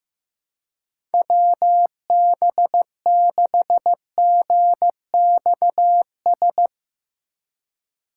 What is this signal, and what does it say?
Morse code, heard as a keyed tone, about 15 wpm, 700 Hz. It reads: WB6GXS
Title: USA Call Signs - 6 Characters - 3x Character Spacing - 15wpm